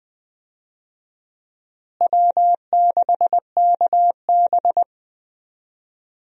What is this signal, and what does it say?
Morse code, heard as a keyed tone, about 20 wpm, 700 Hz. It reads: W6KB
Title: USA Call Signs - 4 Characters - Repeated 3x - 20wpm